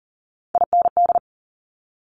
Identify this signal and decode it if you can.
Morse code, heard as a keyed tone, about 39 wpm, 690 Hz. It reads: IND
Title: Top Three Letter Combinations in Words 40wpm